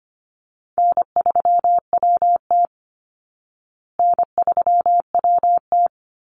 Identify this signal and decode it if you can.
Morse code, heard as a keyed tone, about 25 wpm, 700 Hz. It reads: N3WT N3WT
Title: Call Signs Repeated 3x - 25wpm